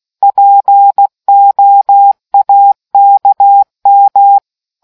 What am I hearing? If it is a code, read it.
Morse code, heard as a keyed tone, about 16 wpm, 800 Hz. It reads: POAKM